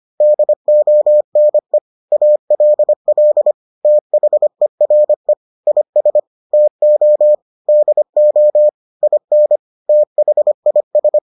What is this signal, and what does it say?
Morse code, heard as a keyed tone, about 25 wpm, 600 Hz. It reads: DONE ALL THERE IS TO DO IN THIS